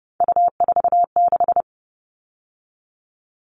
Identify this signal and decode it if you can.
Morse code, heard as a keyed tone, about 30 wpm, 700 Hz. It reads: U46
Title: All Cut Numbers - 3-Digit 30wpm